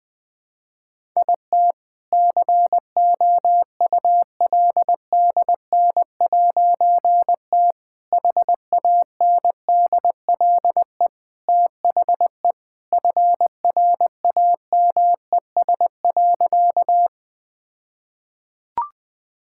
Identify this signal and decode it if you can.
Morse code, heard as a keyed tone, about 20 wpm, 700 Hz. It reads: IT COULDN'T HANDLE THE FRAMES.
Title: All-Age Appropriate Jokes - 20wpm